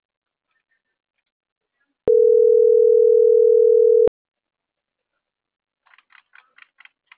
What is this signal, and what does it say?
Ringback tone (North America), 440 + 480 Hz.